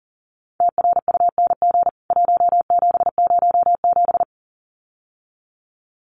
Morse code, heard as a keyed tone, about 40 wpm, 700 Hz. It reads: TRUNG 1707
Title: CWT Exchange 40wpm